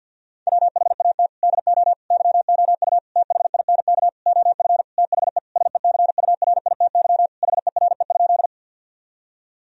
Morse code, heard as a keyed tone, about 50 wpm, 700 Hz. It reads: WHAT DO YOU THINK OF THE SECURITY HERE?
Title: Sentences from Top 1000 Words - Review Part 4 - 50wpm